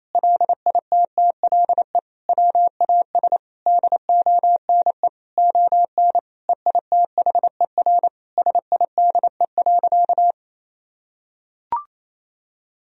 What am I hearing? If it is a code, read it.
Morse code, heard as a keyed tone, about 28 wpm, 700 Hz. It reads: LITTLE WAS DONE ON EITHER SIDE.
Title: Sentences from Top 1000 Words - Review Part 4 - 28wpm